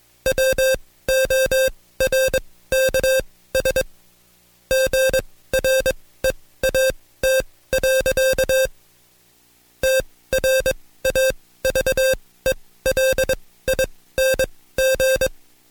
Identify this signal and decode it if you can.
Morse code, heard as a keyed tone, about 22 wpm, spaced as for 16 wpm, 530 Hz. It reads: WORKS GREAT. TRAVELING
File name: Code Quick / AudioDownld2 / Seg134-Stepper7 15WPM